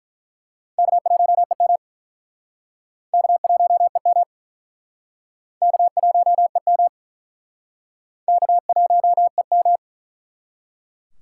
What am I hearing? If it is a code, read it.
Morse code, heard as a keyed tone, about 47 wpm, 700 Hz. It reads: K1EM K1EM K1EM K1EM